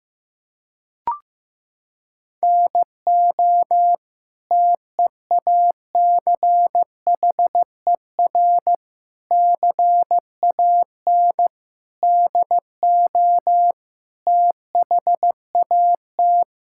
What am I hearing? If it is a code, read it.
Morse code, heard as a keyed tone, about 15 wpm, 700 Hz. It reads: NO TEACHER CAN DO THAT
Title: Sentences from Top 1000 Words - Review Part 6 - 15wpm